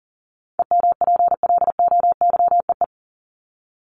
Morse code, heard as a keyed tone, about 38 wpm, 700 Hz. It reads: EMPLOYEE
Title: Top 1501-1750 Words 40wpm